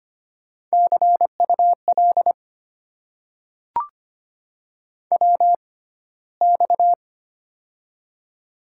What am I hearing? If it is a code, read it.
Morse code, heard as a keyed tone, about 25 wpm, spaced as for 15 wpm, 700 Hz. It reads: CUL WX
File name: Top 90 QSO Elements - 6x Character Spacing - 25wpm